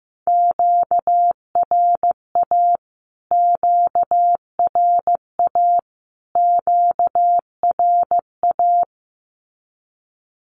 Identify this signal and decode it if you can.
Morse code, heard as a keyed tone, about 15 wpm, 700 Hz. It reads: QRA QRA QRA
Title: Q-Codes Repeated 3x 15wpm